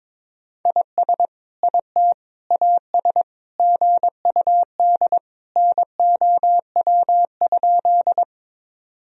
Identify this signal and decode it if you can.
Morse code, heard as a keyed tone, about 22 wpm, 700 Hz. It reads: IS IT AS GUD NOW?
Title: Sentences from Top 200 Words and Abbreviations 22wpm